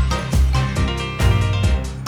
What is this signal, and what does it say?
There is no code.